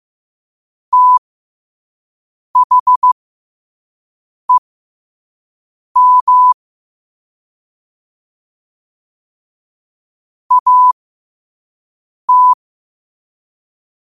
Morse code, heard as a keyed tone, about 15 wpm, spaced as for 5 wpm, 1000 Hz. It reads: THEM AT
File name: qotdfiles / 5wpm